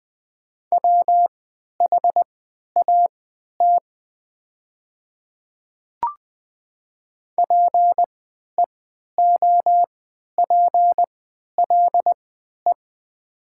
Morse code, heard as a keyed tone, about 20 wpm, spaced as for 11 wpm, 700 Hz. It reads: WHAT PEOPLE
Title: 3x Character Spacing - Top 100 Words - 20wpm